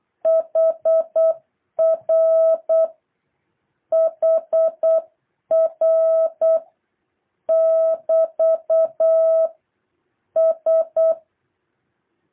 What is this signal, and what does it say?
Morse code, heard as a keyed tone, about 8 wpm, 650 Hz. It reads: HR HR <BT> S
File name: jh8cbh / 2023cw